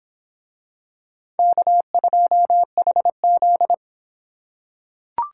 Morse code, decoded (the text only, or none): K2HZ